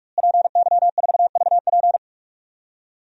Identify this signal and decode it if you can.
Morse code, heard as a keyed tone, about 43 wpm, 700 Hz. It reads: PY4VP
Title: DXCC-100 Call Signs - 5 Characters - Repeated 3x - 45wpm